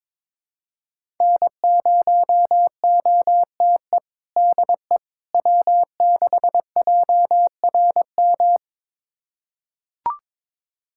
Morse code, heard as a keyed tone, about 22 wpm, 700 Hz. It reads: N0OTE DE W6JRM
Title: USA Callsign DE Callsign - 5 Characters - 22wpm